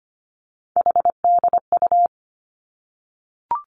Morse code, heard as a keyed tone, about 25 wpm, 700 Hz. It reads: HDU
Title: Three Challenging Letter-Number Combinations 25wpm